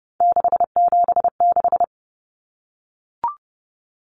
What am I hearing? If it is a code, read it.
Morse code, heard as a keyed tone, about 30 wpm, 700 Hz. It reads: 676
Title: Numbers - 3-Digit 30wpm